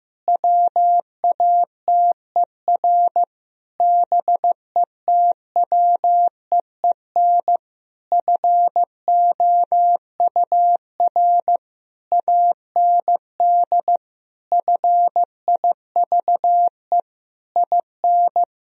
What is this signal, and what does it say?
Morse code, heard as a keyed tone, about 15 wpm, 700 Hz. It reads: WATER BETWEEN FOUR AND FIVE IN
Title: Sentences from Top 1000 Words - Review Part 2 - 15wpm